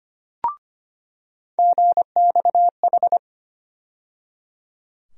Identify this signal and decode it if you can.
Morse code, heard as a keyed tone, about 25 wpm, 700 Hz. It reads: GXH
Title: USA Call Sign Postfixes - 3 Letters 25wpm